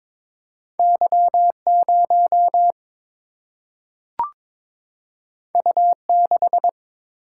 Morse code, heard as a keyed tone, about 22 wpm, 700 Hz. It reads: Y 0 U 6
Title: Single-Letter Single-Number 22wpm